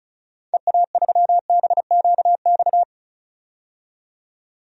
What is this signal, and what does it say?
Morse code, heard as a keyed tone, about 34 wpm, 700 Hz. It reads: EA3BQX
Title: DXCC-100 Call Signs - 6 Characters - 35wpm